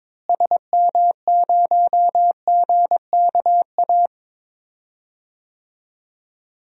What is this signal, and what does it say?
Morse code, heard as a keyed tone, about 22 wpm, 700 Hz. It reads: SM0GKA
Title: DXCC-100 Call Signs - 6 Characters - Repeated 3x - 22wpm